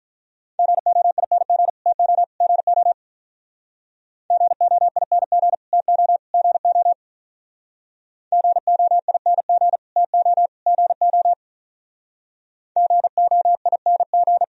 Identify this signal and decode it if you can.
Morse code, heard as a keyed tone, about 44 wpm, 700 Hz. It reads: GOING TO GO GOING TO GO GOING TO GO GOING
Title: Speed-Racing - Sets of 3 Words 35wpm